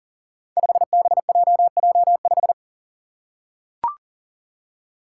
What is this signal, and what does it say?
Morse code, heard as a keyed tone, about 40 wpm, 700 Hz. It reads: 5BJJ5
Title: Five Challenging Letter-Number Combinations 40wpm